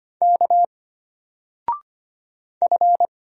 Morse code, heard as a keyed tone, about 25 wpm, 700 Hz. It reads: K F
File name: Single-Letter-Number-25wpm